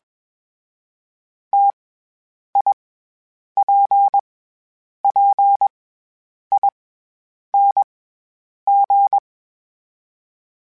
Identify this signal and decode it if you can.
Morse code, heard as a keyed tone, about 21 wpm, 800 Hz. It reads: T I P P I N G
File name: Difficult Combos / Difficult Combos-mp3s / T AOEI @20